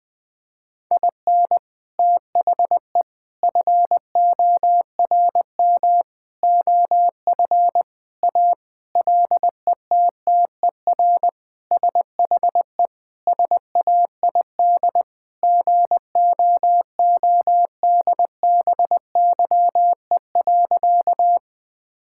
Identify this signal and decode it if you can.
Morse code, heard as a keyed tone, about 21 wpm, 700 Hz. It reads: IN THE FORM OF A LETTER SHE SAID GOODBYE.
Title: Sentences with 4-Word Phrases - 20wpm